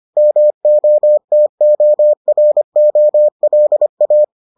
Morse code, heard as a keyed tone, about 25 wpm, 600 Hz. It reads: MOTOROLA